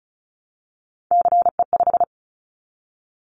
Morse code, heard as a keyed tone, about 36 wpm, 700 Hz. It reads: CE5
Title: Callsign Prefixes for Top 100 DXCC - 35wpm